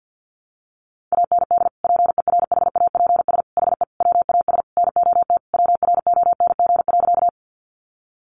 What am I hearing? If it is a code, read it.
Morse code, heard as a keyed tone, about 50 wpm, 700 Hz. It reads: AND PERHAPS HE WAS NOT WRONG.
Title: Sentences from Top 900 Words 50wpm